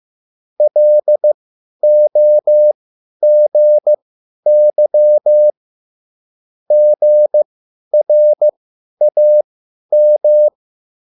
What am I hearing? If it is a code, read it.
Morse code, heard as a keyed tone, about 15 wpm, spaced as for 10 wpm, 600 Hz. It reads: LOGY GRAM